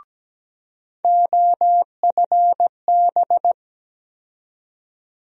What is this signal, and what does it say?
Morse code, heard as a keyed tone, about 17 wpm, 700 Hz. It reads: OFB
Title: USA Call Sign Postfixes - 3 Letters 17wpm